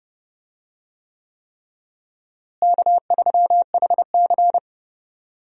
Morse code, heard as a keyed tone, about 30 wpm, 700 Hz. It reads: K3HC
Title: USA Call Signs - 4 Characters - Repeated 3x - 30wpm